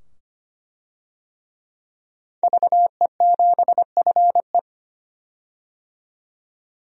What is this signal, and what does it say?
Morse code, heard as a keyed tone, about 25 wpm, 700 Hz. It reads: VE7FE
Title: DXCC-100 Call Signs - 5 Characters - 25wpm